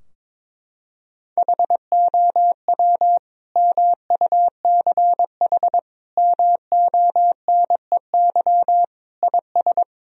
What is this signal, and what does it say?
Morse code, heard as a keyed tone, about 21 wpm, 700 Hz. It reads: HOW MUCH MONEY IS
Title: Sentences from Top 1000 Words - Review Part 4 - 22wpm